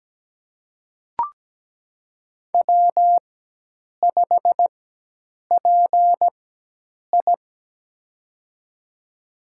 Morse code, heard as a keyed tone, about 17 wpm, spaced as for 8 wpm, 700 Hz. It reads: W5PI
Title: USA Call Signs - 4 Characters - 4x Character Spacing - 17wpm